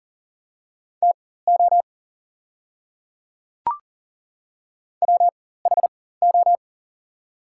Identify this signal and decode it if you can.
Morse code, heard as a keyed tone, about 40 wpm, spaced as for 19 wpm, 690 Hz. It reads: TO WHO
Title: 4x Character Spacing - Top 100 Words - 40wpm